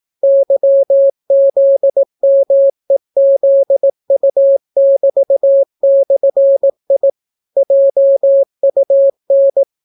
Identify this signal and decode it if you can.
Morse code, heard as a keyed tone, about 18 wpm, 550 Hz. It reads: YZMEZU<BT>/I JUN